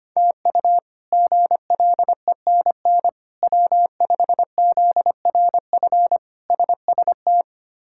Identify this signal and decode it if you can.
Morse code, heard as a keyed tone, about 25 wpm, 700 Hz. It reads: TU GLENN W5ZRF SST
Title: K1USN SST QSO Practice - 25wpm